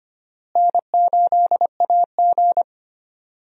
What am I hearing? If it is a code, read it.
Morse code, heard as a keyed tone, about 25 wpm, 700 Hz. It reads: N8AG
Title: USA Call Signs - 4 Characters - Repeated 2x - 25wpm